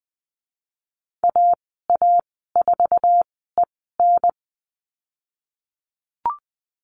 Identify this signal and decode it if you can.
Morse code, heard as a keyed tone, about 20 wpm, spaced as for 15 wpm, 700 Hz. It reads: AA4EN
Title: 055 Lesson 10g - Letter r - Call Signs 20wpm